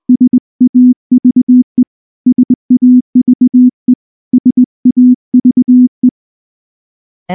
Morse code, heard as a keyed tone, about 19 wpm, 250 Hz. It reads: SAVE SAVE SAVE
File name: morse-training-20WPM